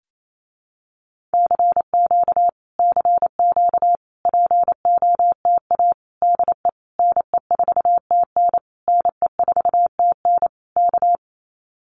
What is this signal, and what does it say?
Morse code, heard as a keyed tone, about 28 wpm, 700 Hz. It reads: CQ CQ POTA DE NE4TN NE4TN K
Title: Common QSO Phrases Encore 28wpm